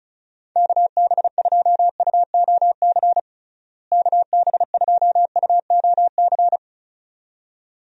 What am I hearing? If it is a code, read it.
Morse code, heard as a keyed tone, about 36 wpm, 700 Hz. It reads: KB2UOC KB2UOC
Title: USA Call Signs - 6 Characters - Repeated 2x - 35wpm